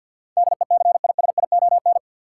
Morse code, heard as a keyed tone, about 52 wpm, 700 Hz. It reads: DECISION
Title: Sentences from Top 1000 Words - New Word Drill - 50wpm